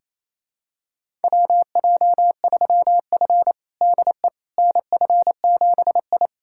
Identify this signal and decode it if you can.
Morse code, heard as a keyed tone, about 28 wpm, 700 Hz. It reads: WJ3F DE NF7I
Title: USA Callsign DE Callsign - 4 Characters - 28wpm